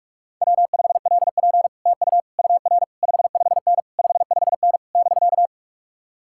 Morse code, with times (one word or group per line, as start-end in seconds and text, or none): 0.4-1.7 W5LP
1.9-2.2 TU
2.4-2.8 UR
3.0-3.8 55N
4.0-4.8 55N
5.0-5.5 <BK>